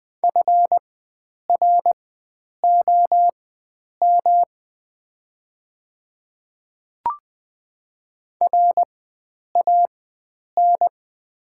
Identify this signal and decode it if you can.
Morse code, heard as a keyed tone, about 20 wpm, spaced as for 9 wpm, 700 Hz. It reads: FROM RAN
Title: 4x Character Spacing - Top 500 Words - 20wpm